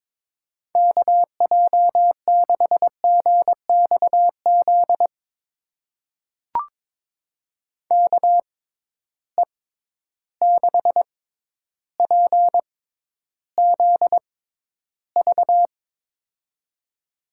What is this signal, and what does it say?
Morse code, heard as a keyed tone, about 22 wpm, spaced as for 11 wpm, 700 Hz. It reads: KJ6GXZ KE6PZV
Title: USA Call Signs - 6 Characters - 6x Character Spacing - 22wpm